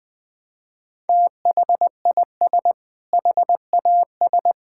Morse code, heard as a keyed tone, about 21 wpm, 700 Hz. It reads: THIS HAS
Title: Sentences with 6-Word Phrases - Spoken Beginning - 20wpm